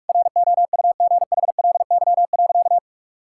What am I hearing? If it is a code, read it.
Morse code, heard as a keyed tone, about 42 wpm, 700 Hz. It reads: ROUGHLY.
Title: Sentences from Top 1600 Words 45wpm